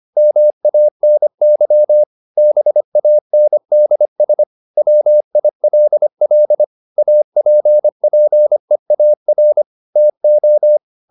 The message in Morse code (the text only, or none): MANY BANDS WILL APPEAR TO